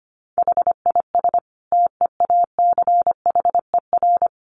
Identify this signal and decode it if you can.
Morse code, heard as a keyed tone, about 25 wpm, 700 Hz. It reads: HIS TEACHER
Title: Additional Easy Phrases 25wpm